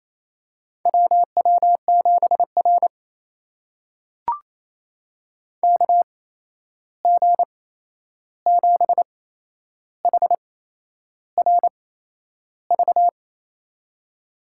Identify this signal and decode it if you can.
Morse code, heard as a keyed tone, about 27 wpm, spaced as for 10 wpm, 700 Hz. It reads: WW7R KG7HRV
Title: USA Call Signs - 8x Character Spacing - 28wpm